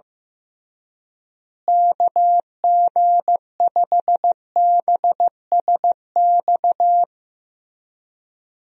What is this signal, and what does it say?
Morse code, heard as a keyed tone, about 16 wpm, 700 Hz. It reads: KG5BSX
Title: USA Call Signs - 6 Characters - Repeated 3x - 15wpm